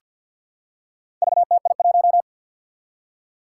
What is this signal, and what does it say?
Morse code, heard as a keyed tone, about 47 wpm, 700 Hz. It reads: VTI1
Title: Four Challenging Letter-Number Combinations 50wpm